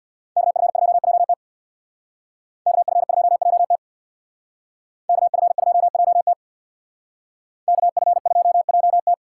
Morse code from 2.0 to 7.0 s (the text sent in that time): KF2JT KF2JT